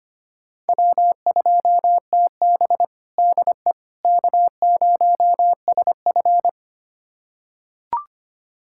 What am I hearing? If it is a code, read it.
Morse code, heard as a keyed tone, about 25 wpm, 700 Hz. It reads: W2TB DE K0SF
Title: USA Callsign DE Callsign - 4 Characters - 25wpm